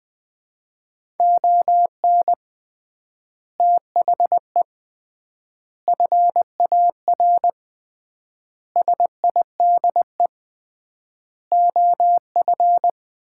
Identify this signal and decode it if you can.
Morse code, heard as a keyed tone, about 21 wpm, 700 Hz. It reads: ON THE FAR SIDE OF
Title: 3x Word-Spacing - Sets of 5 Words - 20wpm